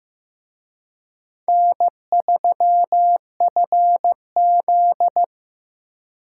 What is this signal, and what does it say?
Morse code, heard as a keyed tone, about 18 wpm, 700 Hz. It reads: N3FZ
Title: Speed Racing - USA 1x2 Call Signs 15wpm